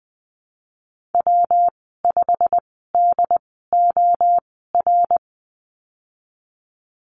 Morse code, heard as a keyed tone, about 20 wpm, spaced as for 15 wpm, 700 Hz. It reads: W5DOR
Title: 147 Lesson 24g - Letter y - Call Signs 20wpm